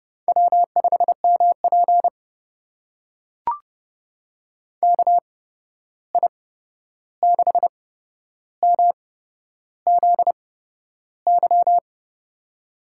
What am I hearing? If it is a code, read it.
Morse code, heard as a keyed tone, about 30 wpm, spaced as for 11 wpm, 700 Hz. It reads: W5MP KI6MZY
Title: USA Call Signs - 8x Character Spacing - 30wpm